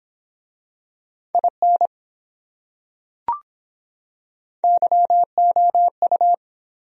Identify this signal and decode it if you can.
Morse code, heard as a keyed tone, about 22 wpm, 700 Hz. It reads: IN YOU